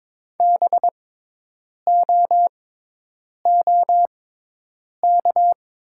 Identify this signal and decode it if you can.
Morse code, heard as a keyed tone, about 22 wpm, spaced as for 8 wpm, 700 Hz. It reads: BOOK
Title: 6x Character Spacing - Top 500 Words - 22wpm